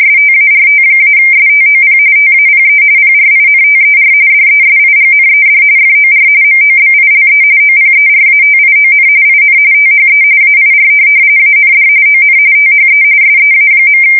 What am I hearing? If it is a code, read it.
RTTY, about 45.45 baud. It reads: START-STOP TAPE PRINTER DESIGNED FOR MOBILE OPERATION IN THE AIR AND ON THE SEA.
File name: fsk170 / teletype-corp